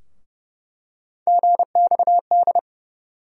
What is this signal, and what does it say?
Morse code, heard as a keyed tone, about 30 wpm, 700 Hz. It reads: GXD
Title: USA Call Sign Postfixes - 3 Letters 30wpm